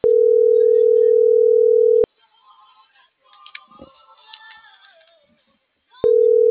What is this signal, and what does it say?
Ringback tone (North America), 440 + 480 Hz.